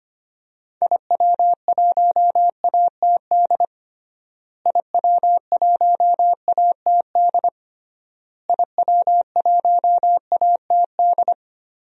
Morse code, heard as a keyed tone, about 25 wpm, 700 Hz. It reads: IW1ATD IW1ATD IW1ATD
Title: DXCC-100 Call Signs - 6 Characters - Repeated 3x - 25wpm